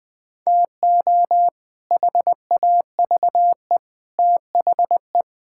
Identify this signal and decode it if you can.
Morse code, heard as a keyed tone, about 20 wpm, 700 Hz. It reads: TO HAVE THE
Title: Build A Phrase - 3-Word Phrases - 20wpm